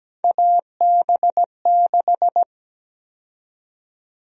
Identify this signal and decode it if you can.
Morse code, heard as a keyed tone, about 17 wpm, 700 Hz. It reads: AB6
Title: USA Call Sign Prefixes - Repeated 2x - 17wpm